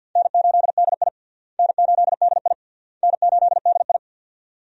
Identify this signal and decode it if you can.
Morse code, heard as a keyed tone, about 51 wpm, 700 Hz. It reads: N8DI N8DI N8DI
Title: USA Call Signs - 4 Characters - Repeated 3x - 50wpm